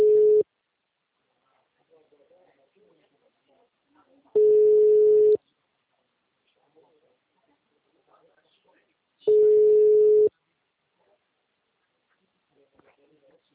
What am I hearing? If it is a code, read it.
Ringback tone (Europe, 425 Hz), with speech.